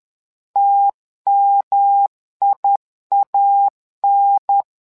Morse code, heard as a keyed tone, about 11 wpm, 800 Hz. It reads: TMIAN
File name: LEZIONI / 3 MWRO / 3 MWRO 10wpm 2148